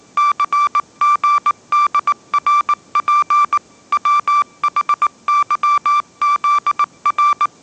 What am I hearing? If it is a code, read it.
Morse code, heard as a keyed tone, about 19 wpm, 1170 Hz. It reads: CGDRP WHYZR